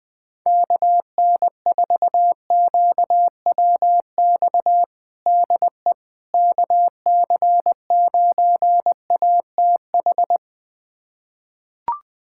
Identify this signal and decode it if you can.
Morse code, heard as a keyed tone, about 21 wpm, 700 Hz. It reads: KN4QWX DE KC9ATH
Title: USA Callsign DE Callsign - 6 Characters - 20wpm